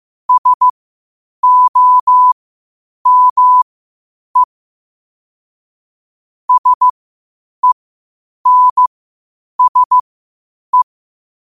Morse code, heard as a keyed tone, about 15 wpm, spaced as for 8 wpm, 1000 Hz. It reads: SOME SENSE
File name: qotdfiles / 8wpm